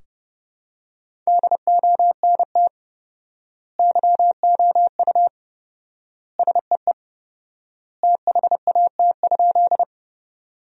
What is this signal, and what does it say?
Morse code, heard as a keyed tone, about 30 wpm, 700 Hz. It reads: DONT YOU SEE THAT?
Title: 4x Word-Spacing - Sentences from Top 300 Words - 30wpm